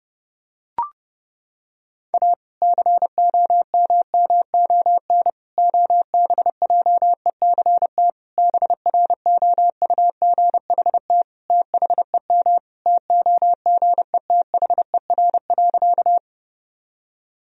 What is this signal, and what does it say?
Morse code, heard as a keyed tone, about 31 wpm, 700 Hz. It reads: A COMMON OBJECT BROUGHT THEM TOGETHER.
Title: Sentences from Top 1000 Words - Review Part 3 - 30wpm